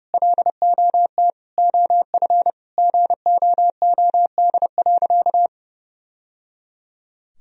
Morse code, heard as a keyed tone, about 30 wpm, 700 Hz. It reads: LOT OF GOOD.
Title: Sentences from Top 1000 Words - Review Part 3 - 30wpm